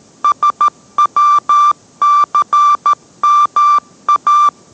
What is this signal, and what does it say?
Morse code, heard as a keyed tone, about 13 wpm, 1180 Hz. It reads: SWCMA